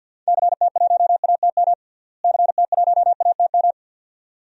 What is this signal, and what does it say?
Morse code, heard as a keyed tone, about 47 wpm, 700 Hz. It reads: CT1ATM CT1ATM
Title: DXCC-100 Call Signs - 6 Characters - Repeated 2x - 50wpm